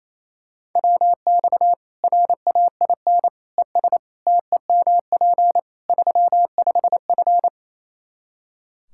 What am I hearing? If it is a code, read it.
Morse code, heard as a keyed tone, about 28 wpm, 700 Hz. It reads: WX RAIN ES TEMP 35F
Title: Practice QSO - Exchange 2 - Info - 28wpm